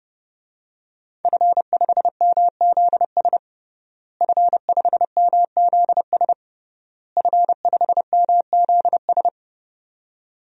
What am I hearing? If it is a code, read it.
Morse code, heard as a keyed tone, about 30 wpm, 700 Hz. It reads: F5MZS F5MZS F5MZS
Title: DXCC-100 Call Signs - 5 Characters - Repeated 3x - 30wpm